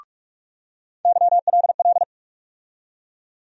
Morse code, heard as a keyed tone, about 46 wpm, 700 Hz. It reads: YLL
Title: USA Call Sign Postfixes - 3 Letters 45wpm